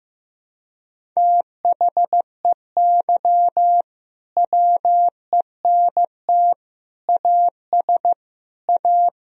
Morse code, heard as a keyed tone, about 15 wpm, 700 Hz. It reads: THEY WENT AS A
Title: Sentences from Top 1000 Words - Review Part 1 - 15wpm